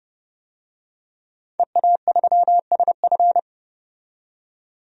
Morse code, heard as a keyed tone, about 30 wpm, 700 Hz. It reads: EA3SF
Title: DXCC-100 Call Signs - 5 Characters - 30wpm